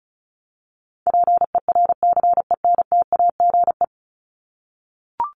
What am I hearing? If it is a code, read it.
Morse code, heard as a keyed tone, about 34 wpm, 700 Hz. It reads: PERCENTAGE
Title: Top 1001-1250 Words 35wpm